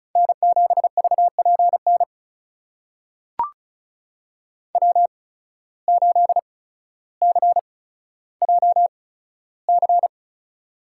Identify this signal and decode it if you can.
Morse code, heard as a keyed tone, about 36 wpm, spaced as for 15 wpm, 700 Hz. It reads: N7VPN W8CJC
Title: USA Call Signs - 5 Characters - 8x Character Spacing - 35wpm